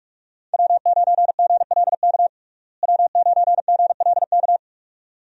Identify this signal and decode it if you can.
Morse code, heard as a keyed tone, about 46 wpm, 700 Hz. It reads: W9GLK W9GLK
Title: USA Call Signs - 5 Characters - Repeated 2x - 45wpm